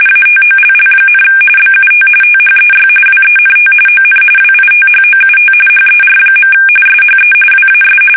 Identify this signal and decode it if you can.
RTTY, about 45.45 baud. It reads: ATOLL, BUT THEY WENT ON WITHOUT HER. SHE WAS
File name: fsk850 / battleshipnj